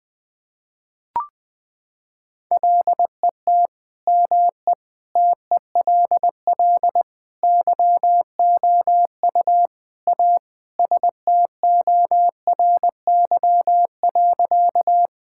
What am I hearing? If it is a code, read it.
Morse code, heard as a keyed tone, about 20 wpm, 700 Hz. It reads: LET ME TELL YOU A STORY.
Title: Sentences with 4-Word Phrases - 20wpm